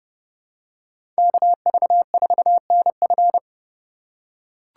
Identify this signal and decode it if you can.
Morse code, heard as a keyed tone, about 32 wpm, 700 Hz. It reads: KV4NF
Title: USA Call Signs - 5 Characters - 2x Character Spacing - 30wpm